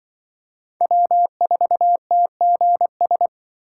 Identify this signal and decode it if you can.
Morse code, heard as a keyed tone, about 25 wpm, 700 Hz. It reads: W4TGS
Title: Speed Racing - USA 1x3 Call Signs 20wpm